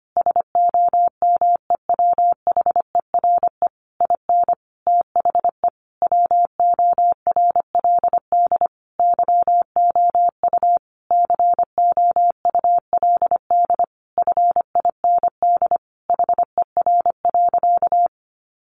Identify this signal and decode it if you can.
Morse code, heard as a keyed tone, about 25 wpm, 700 Hz. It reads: SOMEWHERE IN THE WORLD YOU COULD FIND HER.
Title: Sentences from Top 1400 Words 25wpm